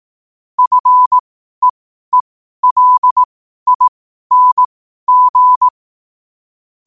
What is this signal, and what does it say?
Morse code, heard as a keyed tone, about 18 wpm, spaced as for 12 wpm, 1000 Hz. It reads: FEELING
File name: qotdfiles / 12wpm